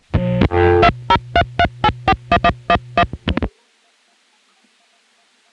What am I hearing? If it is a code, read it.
DTMF, heard as telephone touch tones, tones of about 60 ms. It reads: 2033841111